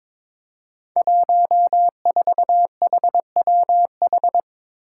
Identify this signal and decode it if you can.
Morse code, heard as a keyed tone, about 22 wpm, 700 Hz. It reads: 14HWH
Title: Five Challenging Letter-Number Combinations 22wpm